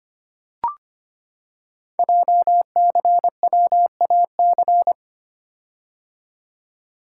Morse code, heard as a keyed tone, about 25 wpm, 700 Hz. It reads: JCWAC
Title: Code Groups - 5 Letters - 25wpm